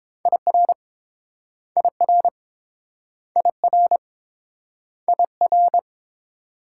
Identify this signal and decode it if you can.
Morse code, heard as a keyed tone, about 29 wpm, 700 Hz. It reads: IR IR IR IR